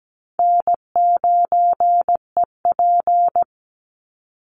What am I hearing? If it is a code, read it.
Morse code, heard as a keyed tone, about 17 wpm, 700 Hz. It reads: N9EP
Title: Easy Call Signs Repeated 3x - 17wpm